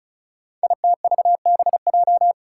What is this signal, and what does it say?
Morse code, heard as a keyed tone, about 36 wpm, 700 Hz. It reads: ITVBJ
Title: Code Groups - 5 Letters - 35wpm